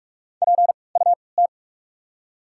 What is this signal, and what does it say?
Morse code, heard as a keyed tone, about 46 wpm, spaced as for 25 wpm, 700 Hz. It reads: PUT
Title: 3x Character Spacing - Top 300 Words - 45wpm